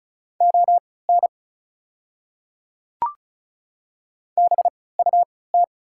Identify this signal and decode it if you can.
Morse code, heard as a keyed tone, about 36 wpm, spaced as for 20 wpm, 690 Hz. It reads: ON BUT
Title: 3x Character Spacing - Top 100 Words - 35wpm